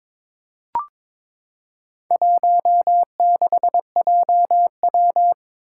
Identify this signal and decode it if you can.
Morse code, heard as a keyed tone, about 22 wpm, 700 Hz. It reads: 16JW
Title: Four Challenging Letter-Number Combinations 22wpm